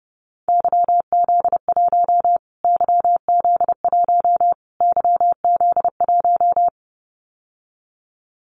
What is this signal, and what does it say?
Morse code, heard as a keyed tone, about 30 wpm, 700 Hz. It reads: YZ1 YZ1 YZ1
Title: Callsign Prefixes for Top 25 DXCC - Repeated 3x - 30wpm